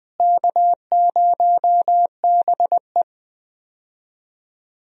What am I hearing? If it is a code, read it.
Morse code, heard as a keyed tone, about 19 wpm, 700 Hz. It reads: K0BE
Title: USA Call Signs - 4 Characters - Repeated 3x - 20wpm